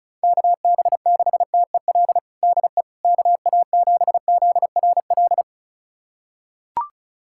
Morse code, heard as a keyed tone, about 36 wpm, 700 Hz. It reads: KB6TEL DE KA7ZRL
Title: USA Callsign DE Callsign 35wpm